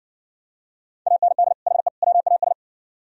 Morse code, heard as a keyed tone, about 60 wpm, 700 Hz. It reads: AND HE WAS